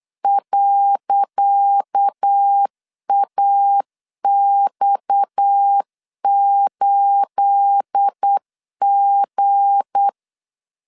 Morse code, heard as a keyed tone, about 8 wpm, 800 Hz. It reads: .AX8G
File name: LEZIONI / 10 SIGNS / 10C DOT 8wpm 3649